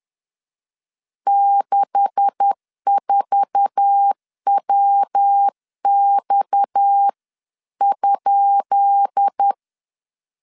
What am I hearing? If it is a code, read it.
Morse code, heard as a keyed tone, about 10 wpm, 800 Hz. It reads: 64WX?
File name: LEZIONI / 10 SIGNS / 10B INT 10wpm 2844